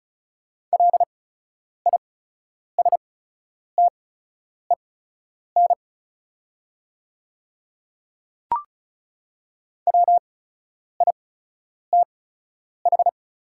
Morse code, heard as a keyed tone, about 36 wpm, spaced as for 10 wpm, 700 Hz. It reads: LISTEN WITH
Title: 8x Character Spacing - Top 500 Words - 35wpm